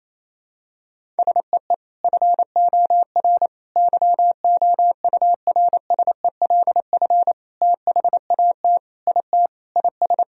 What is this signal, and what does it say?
Morse code, heard as a keyed tone, about 28 wpm, 700 Hz. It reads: SEE FOR YOURSELF THAT IT IS
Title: Sentences from Top 1000 Words - Review Part 1 - 28wpm